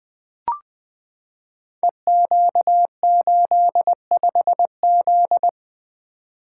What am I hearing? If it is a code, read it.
Morse code, heard as a keyed tone, about 20 wpm, 700 Hz. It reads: EQ85Z
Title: Code Groups - 5 Letter:Numbers - 20wpm